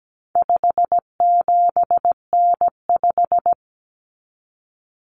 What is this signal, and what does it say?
Morse code, heard as a keyed tone, about 17 wpm, 700 Hz. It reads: 57N5
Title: Common Cut Numbers - 4-Digit 17wpm